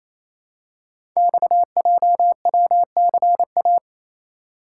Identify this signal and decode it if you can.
Morse code, heard as a keyed tone, about 28 wpm, 690 Hz. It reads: XJWCA
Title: Code Groups - 5 Letter:Numbers - 28wpm